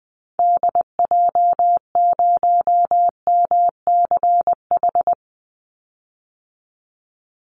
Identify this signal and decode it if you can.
Morse code, heard as a keyed tone, about 20 wpm, 700 Hz. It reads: DJ0MCH
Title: Call Signs Repeated 3x - 20wpm